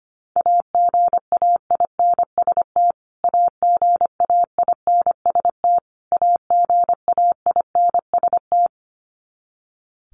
Morse code, heard as a keyed tone, about 25 wpm, 700 Hz. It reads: AGAINST AGAINST AGAINST
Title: Top 500 Words Repeated 25wpm